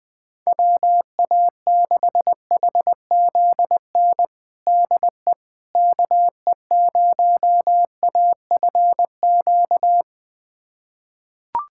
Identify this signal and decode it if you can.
Morse code, heard as a keyed tone, about 21 wpm, 700 Hz. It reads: WA6HZN DE KE0AFQ
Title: USA Callsign DE Callsign - 6 Characters - 20wpm